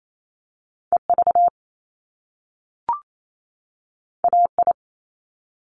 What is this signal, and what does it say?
Morse code, heard as a keyed tone, about 28 wpm, 700 Hz. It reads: EV AI
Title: Top Two Letter Combinations in Words 28wpm